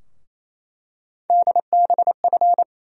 Morse code, heard as a keyed tone, about 27 wpm, 700 Hz. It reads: DBF
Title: USA Call Sign Postfixes - 3 Letters 28wpm